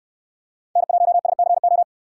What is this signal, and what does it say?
Morse code, heard as a keyed tone, about 67 wpm, 700 Hz. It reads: N1SCO